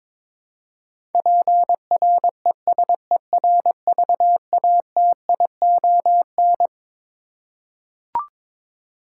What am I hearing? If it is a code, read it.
Morse code, heard as a keyed tone, about 22 wpm, 700 Hz. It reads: PRESERVATION
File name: Twelve Letter Words - 22wpm